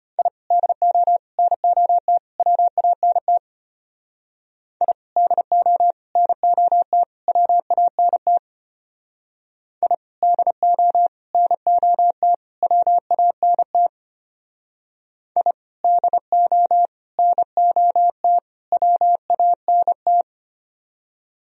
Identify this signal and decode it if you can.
Morse code, heard as a keyed tone, about 32 wpm, 700 Hz. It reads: I DO NOT WANT I DO NOT WANT I DO NOT WANT I DO NOT WANT